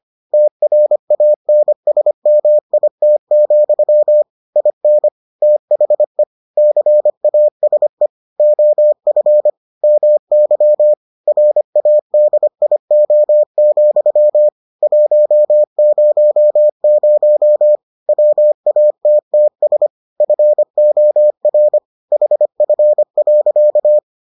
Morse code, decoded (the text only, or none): TRANSMIT, IN THE CASE OF MY RADIO, 100 WATTS FOR HF.